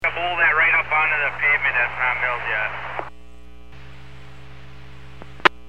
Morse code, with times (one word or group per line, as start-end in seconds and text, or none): none